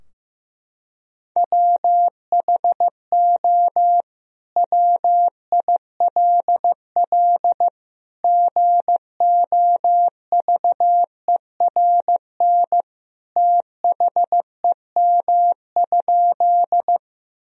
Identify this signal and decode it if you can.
Morse code, heard as a keyed tone, about 15 wpm, 700 Hz. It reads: WHO WILL GOVERN THEM?
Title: Sentences from Top 1000 Words - Review Part 2 - 15wpm